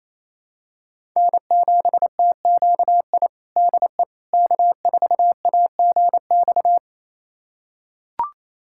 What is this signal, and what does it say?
Morse code, heard as a keyed tone, about 28 wpm, 700 Hz. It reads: N7TQI DE K4AGX
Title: USA Callsign DE Callsign 28wpm